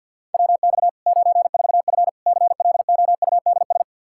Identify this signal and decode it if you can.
Morse code, heard as a keyed tone, about 53 wpm, 700 Hz. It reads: WX 94F CLOUDS
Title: Practice QSO - Exchange 2 - Info - 50wpm